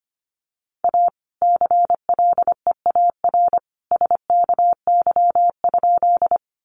Morse code, heard as a keyed tone, about 25 wpm, 700 Hz. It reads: A CLEAR SKY?
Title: Sentences from Top 1200 Words 25wpm